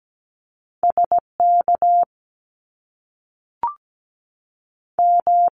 Morse code, heard as a keyed tone, about 17 wpm, 700 Hz. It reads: SK M
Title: Top Two Letter Combinations in Words 17wpm